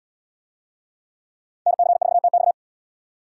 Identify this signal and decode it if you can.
Morse code, heard as a keyed tone, about 71 wpm, 700 Hz. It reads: NC4TX